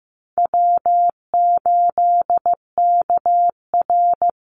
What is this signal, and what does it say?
Morse code, heard as a keyed tone, about 15 wpm, 700 Hz. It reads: W8KR
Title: Easy Call Signs - 15wpm